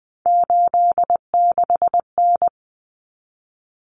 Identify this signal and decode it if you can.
Morse code, heard as a keyed tone, about 20 wpm, 700 Hz. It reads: 86N
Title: Common Cut Numbers - 3-Digit 20wpm